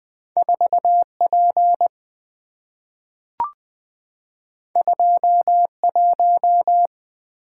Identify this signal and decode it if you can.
Morse code, heard as a keyed tone, about 20 wpm, 700 Hz. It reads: 4P 21
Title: Two Letter-Number Combinations - 20wpm